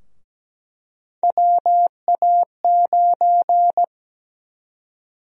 Morse code, heard as a keyed tone, about 17 wpm, 700 Hz. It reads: WA9
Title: USA Call Sign Prefixes - 17wpm